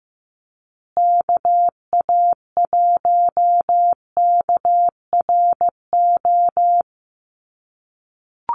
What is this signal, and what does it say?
Morse code, heard as a keyed tone, about 15 wpm, 700 Hz. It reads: KA1KRO
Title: USA Call Signs Repeated 2x 15wpm